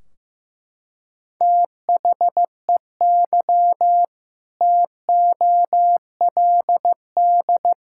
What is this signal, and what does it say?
Morse code, heard as a keyed tone, about 16 wpm, 700 Hz. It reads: THEY TOLD
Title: Sentences from Top 1600 Words 15wpm